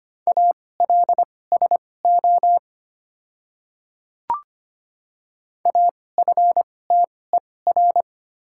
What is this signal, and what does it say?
Morse code, heard as a keyed tone, about 25 wpm, spaced as for 18 wpm, 700 Hz. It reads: ALSO AFTER
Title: 2x Character Spacing - Top 200 Words - 25wpm